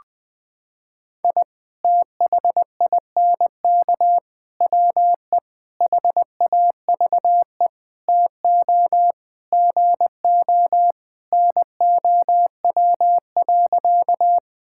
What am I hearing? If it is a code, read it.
Morse code, heard as a keyed tone, about 20 wpm, 700 Hz. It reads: I THINK WE HAVE TO GO NOW.
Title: Sentences with 5-Word Phrases - 20wpm